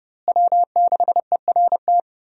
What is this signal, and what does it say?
Morse code, heard as a keyed tone, about 30 wpm, 700 Hz. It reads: W6ERT
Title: USA Call Signs - 5 Characters - Repeated 2x - 30wpm